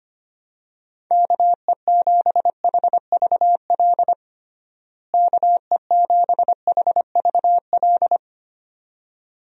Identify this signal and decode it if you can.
Morse code, heard as a keyed tone, about 25 wpm, 700 Hz. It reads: KE7HVL KE7HVL
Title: USA Call Signs - 6 Characters - Repeated 2x - 25wpm